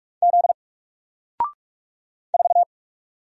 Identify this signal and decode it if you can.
Morse code, heard as a keyed tone, about 44 wpm, 700 Hz. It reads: Z 4
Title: Single Letter-Number 45wpm